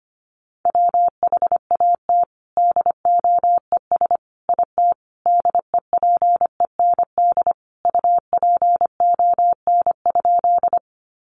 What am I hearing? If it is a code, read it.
Morse code, heard as a keyed tone, about 25 wpm, 700 Hz. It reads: WHAT DOES IT DEPEND UPON?
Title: Sentences from Top 1300 Words 25wpm